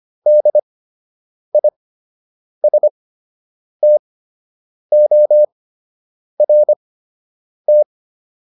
Morse code, heard as a keyed tone, about 25 wpm, spaced as for 8 wpm, 600 Hz. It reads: DISTORT